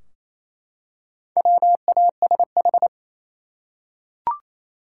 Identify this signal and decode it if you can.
Morse code, heard as a keyed tone, about 28 wpm, 700 Hz. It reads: WASH
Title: Four Letter Words - 28wpm